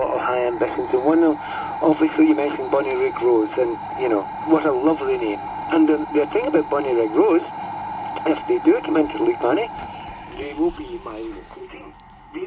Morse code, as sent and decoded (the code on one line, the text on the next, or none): none